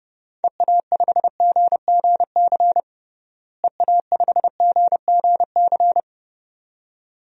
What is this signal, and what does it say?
Morse code, heard as a keyed tone, about 30 wpm, 700 Hz. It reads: EA5GGC EA5GGC
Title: DXCC-100 Call Signs - 6 Characters - Repeated 2x - 30wpm